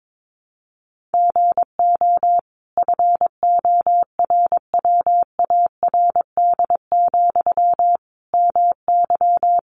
Morse code, heard as a keyed tone, about 22 wpm, 700 Hz. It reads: GO FORWARD, MY
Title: Sentences from Top 1300 Words 22wpm